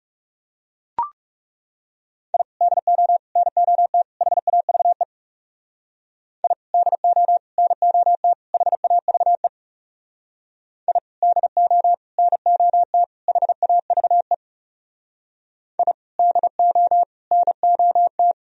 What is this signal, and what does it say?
Morse code, heard as a keyed tone, about 36 wpm, 700 Hz. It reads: I DO NOT HAVE I DO NOT HAVE I DO NOT HAVE I DO NOT
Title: Speed-Racing - Sets of 4 Words 30wpm